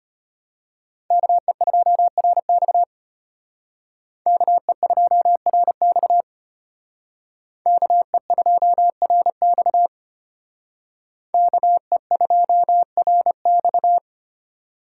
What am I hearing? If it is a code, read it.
Morse code, heard as a keyed tone, about 33 wpm, 700 Hz. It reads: KE2RX KE2RX KE2RX KE2RX